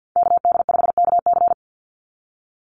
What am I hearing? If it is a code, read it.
Morse code, heard as a keyed tone, about 48 wpm, 700 Hz. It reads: KD5KC
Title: NA SOTA Chaser Callsigns 50wpm